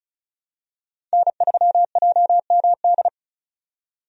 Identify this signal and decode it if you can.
Morse code, heard as a keyed tone, about 36 wpm, 700 Hz. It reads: N3JMD
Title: USA Call Signs - 5 Characters - 35wpm